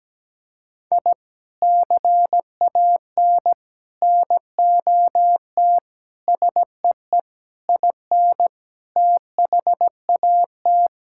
Morse code, heard as a keyed tone, about 17 wpm, 700 Hz. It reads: I CAN NOT SEE IN THAT
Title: Sentences from Top 1000 Words - Review Part 5 - 17wpm